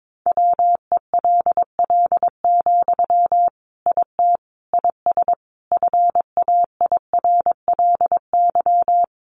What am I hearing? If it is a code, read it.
Morse code, heard as a keyed tone, about 22 wpm, 700 Hz. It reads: WELL, IT IS FAIRLY
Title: Sentences from Top 1100 Words 22wpm